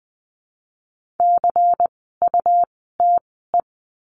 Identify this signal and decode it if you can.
Morse code, heard as a keyed tone, about 20 wpm, spaced as for 14 wpm, 700 Hz. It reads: CUTE
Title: 101 Lesson 17f - Letter u - Words 20wpm